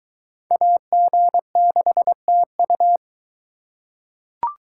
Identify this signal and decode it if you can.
Morse code, heard as a keyed tone, about 22 wpm, 700 Hz. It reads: AG6TU